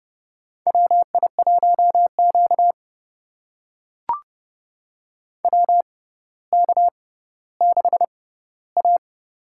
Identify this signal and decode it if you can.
Morse code, heard as a keyed tone, about 30 wpm, spaced as for 15 wpm, 700 Hz. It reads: WI1Q WK6A
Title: USA Call Signs - 4 Characters - 6x Character Spacing - 30wpm